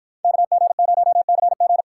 Morse code, heard as a keyed tone, about 53 wpm, 700 Hz. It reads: KG0CG